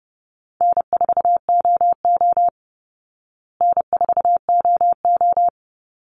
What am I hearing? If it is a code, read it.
Morse code, heard as a keyed tone, about 29 wpm, 700 Hz. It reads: N4OO N4OO
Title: Call Signs Repeated 3x - 30wpm